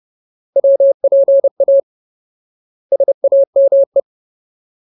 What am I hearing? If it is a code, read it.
Morse code, heard as a keyed tone, about 30 wpm, 550 Hz. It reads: WPA SAME
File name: Ultimate Word List / 30WPM Ultimate